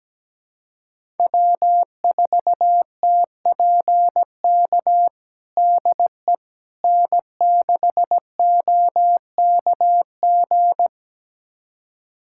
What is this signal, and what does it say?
Morse code, heard as a keyed tone, about 17 wpm, 700 Hz. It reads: W4TPK DE N6OKG
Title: USA Callsign DE Callsign - 5 Characters - 17wpm